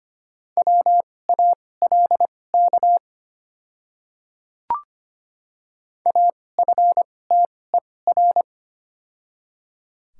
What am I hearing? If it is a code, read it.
Morse code, heard as a keyed tone, about 25 wpm, spaced as for 18 wpm, 700 Hz. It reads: WALK AFTER
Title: 2x Character Spacing - Top 500 Words - 25wpm